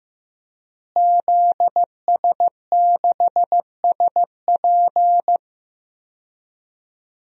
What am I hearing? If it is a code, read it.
Morse code, heard as a keyed tone, about 15 wpm, 700 Hz. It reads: ZS6SP
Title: DXCC-100 Call Signs - 5 Characters - Repeated 3x - 15wpm